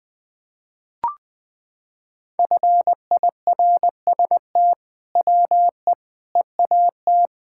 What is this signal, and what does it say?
Morse code, heard as a keyed tone, about 20 wpm, 700 Hz. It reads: FIRST WE EAT
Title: Sentences with 2-Word Phrases - 20wpm